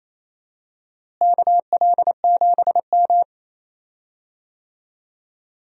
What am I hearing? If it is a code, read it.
Morse code, heard as a keyed tone, about 28 wpm, 700 Hz. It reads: KL7M
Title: DXCC-100 Call Signs - 4 Characters - Repeated 2x - 28wpm